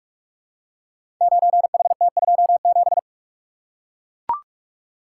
Morse code, heard as a keyed tone, about 44 wpm, 700 Hz. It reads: 9HT27
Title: Code Groups - 5 Letter:Numbers - 45wpm